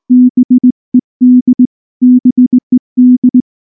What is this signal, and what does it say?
Morse code, heard as a keyed tone, about 20 wpm, 250 Hz. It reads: BED BED
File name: morse / morse-training-20WPM-10